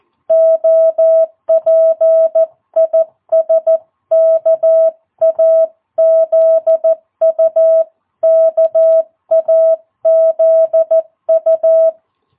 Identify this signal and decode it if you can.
Morse code, heard as a keyed tone, about 14 wpm, 660 Hz. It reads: OP IS KAZU KAZU
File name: jh8cbh / 2022cw